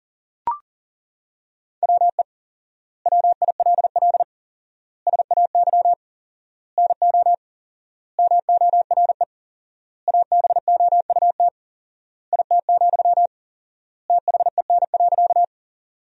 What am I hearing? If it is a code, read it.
Morse code, heard as a keyed tone, about 40 wpm, 700 Hz. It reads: WE WILL SAY NO MORE ABOUT IT, THEN.
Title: 4x Word-Spacing - Sentences from Top 200 Words - 40wpm